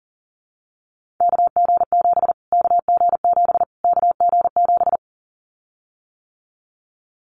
Morse code, heard as a keyed tone, about 37 wpm, 700 Hz. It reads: KG7 KG7 KG7
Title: Callsign Prefixes for Top 25 DXCC - Repeated 3x - 40wpm